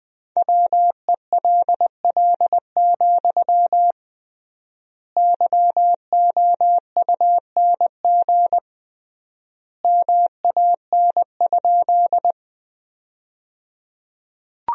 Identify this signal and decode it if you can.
Morse code, heard as a keyed tone, about 20 wpm, 700 Hz. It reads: WELL, YOUNG MAN?
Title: 3x Word-Spacing - Sentences from Top 400 Words - 20wpm